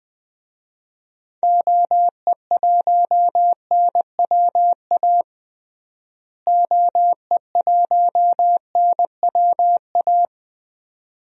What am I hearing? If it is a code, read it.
Morse code, heard as a keyed tone, about 20 wpm, 700 Hz. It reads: OE1NWA OE1NWA
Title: DXCC-100 Call Signs - 6 Characters - Repeated 2x - 20wpm